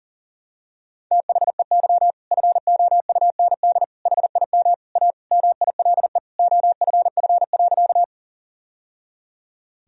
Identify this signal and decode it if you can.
Morse code, heard as a keyed tone, about 40 wpm, 700 Hz. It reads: THEY FOUND HIM A MILE OFF.
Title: Sentences from Top 1000 Words - Review Part 1 - 40wpm